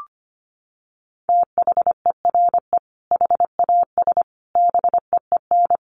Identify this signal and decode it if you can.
Morse code, heard as a keyed tone, about 25 wpm, 700 Hz. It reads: THERE HAS BEEN
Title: Sentences from Top 1500 Words 25wpm